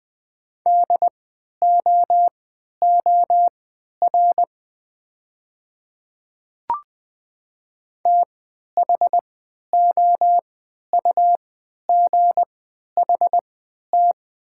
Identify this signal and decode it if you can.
Morse code, heard as a keyed tone, about 20 wpm, spaced as for 11 wpm, 700 Hz. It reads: DOOR THOUGHT
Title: 3x Character Spacing - Top 500 Words - 20wpm